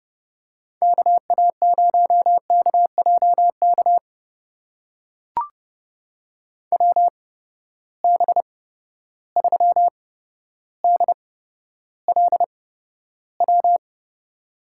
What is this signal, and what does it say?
Morse code, heard as a keyed tone, about 30 wpm, spaced as for 13 wpm, 700 Hz. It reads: KA0KJK WB3DLW
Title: USA Call Signs - 6 Characters - 8x Character Spacing - 30wpm